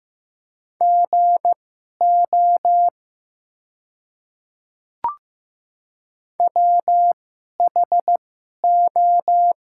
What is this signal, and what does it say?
Morse code, heard as a keyed tone, about 15 wpm, spaced as for 11 wpm, 700 Hz. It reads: GO WHO
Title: 2x Character Spacing - Top 200 Words - 15wpm